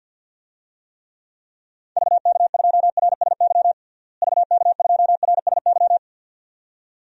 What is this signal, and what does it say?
Morse code, heard as a keyed tone, about 50 wpm, 700 Hz. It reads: VK2RSY VK2RSY
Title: DXCC-100 Call Signs - 6 Characters - Repeated 2x - 50wpm